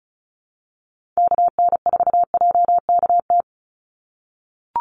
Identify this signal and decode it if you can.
Morse code, heard as a keyed tone, about 34 wpm, 700 Hz. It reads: KN4JKT
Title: USA Call Signs 35wpm